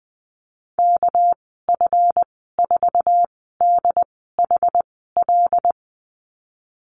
Morse code, heard as a keyed tone, about 20 wpm, spaced as for 15 wpm, 700 Hz. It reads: KF4DHL
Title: 231 Lesson 37g - Letter x - Call Signs 20wpm